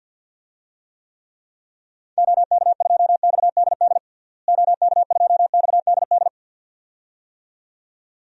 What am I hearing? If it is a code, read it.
Morse code, heard as a keyed tone, about 52 wpm, 700 Hz. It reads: OK2XDD OK2XDD
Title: DXCC-100 Call Signs - 6 Characters - Repeated 2x - 50wpm